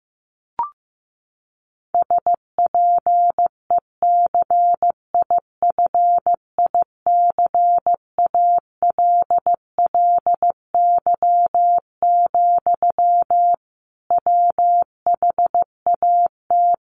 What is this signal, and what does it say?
Morse code, heard as a keyed tone, about 15 wpm, 700 Hz. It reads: SPECIFICALLY, WHAT
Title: Sentences from Top 1100 Words 15wpm